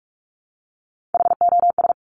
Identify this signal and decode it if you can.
Morse code, heard as a keyed tone, about 46 wpm, 700 Hz. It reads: HOS
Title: Top Three Letter Combinations in Words 45wpm